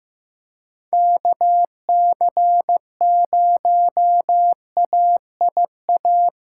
Morse code, heard as a keyed tone, about 15 wpm, 700 Hz. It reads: KC0AIA
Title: USA Call Signs - 6 Characters - 6x Character Spacing - 15wpm